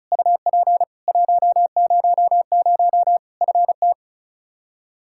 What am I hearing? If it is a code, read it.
Morse code, heard as a keyed tone, about 36 wpm, 700 Hz. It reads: UP 100 FT
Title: Practice QSO - Exchange 2 - Info - 35wpm